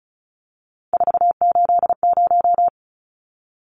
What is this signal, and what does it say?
Morse code, heard as a keyed tone, about 34 wpm, 700 Hz. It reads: 480
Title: Numbers - 3-Digit 35wpm